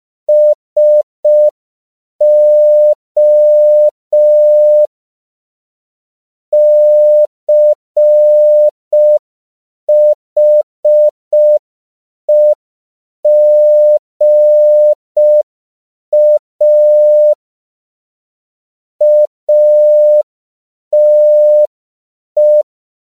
Morse code, heard as a keyed tone, about 5 wpm, 600 Hz. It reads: SO CHEGA ATE
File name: CW / 5PPM